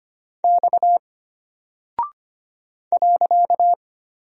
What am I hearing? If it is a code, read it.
Morse code, heard as a keyed tone, about 25 wpm, 700 Hz. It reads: X .
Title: Single Character (Letter:Number:Punctuation:Prosign) Alt 25wpm